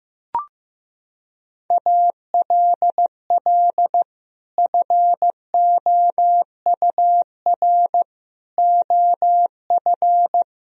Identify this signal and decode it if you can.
Morse code, heard as a keyed tone, about 15 wpm, 700 Hz. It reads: ALL FOUR OF
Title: Sentences from Top 1000 Words - Review Part 1 - 15wpm